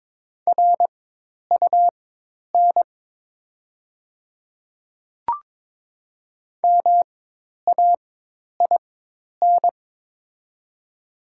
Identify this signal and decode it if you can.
Morse code, heard as a keyed tone, about 22 wpm, spaced as for 10 wpm, 700 Hz. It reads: RUN MAIN
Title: 4x Character Spacing - Top 300 Words - 22wpm